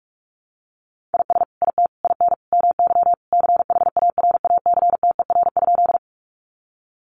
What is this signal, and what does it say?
Morse code, heard as a keyed tone, about 44 wpm, 700 Hz. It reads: IS IT IN MY CHARACTER?
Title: Sentences from Top 1200 Words 45wpm